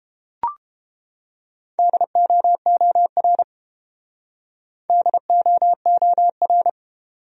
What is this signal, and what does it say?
Morse code, heard as a keyed tone, about 30 wpm, 700 Hz. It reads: DOOR DOOR